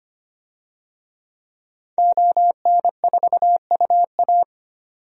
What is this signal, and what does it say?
Morse code, heard as a keyed tone, about 25 wpm, 700 Hz. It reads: ON4UA
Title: DXCC-100 Call Signs - 5 Characters - 25wpm